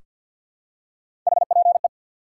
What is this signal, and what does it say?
Morse code, heard as a keyed tone, about 47 wpm, 700 Hz. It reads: HPE
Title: Top 180 QSO Elements 50wpm